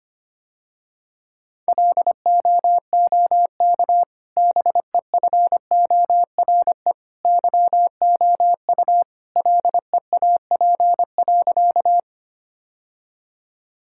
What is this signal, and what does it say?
Morse code, heard as a keyed tone, about 25 wpm, 700 Hz. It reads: LOOK BEFORE YOU LEAP.
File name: Easy Sentences with Hints - 25wpm